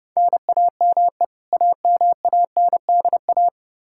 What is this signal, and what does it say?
Morse code, heard as a keyed tone, about 31 wpm, 700 Hz. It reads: NAME AMANDA
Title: Practice QSO - Exchange 1 - Beginning - 30wpm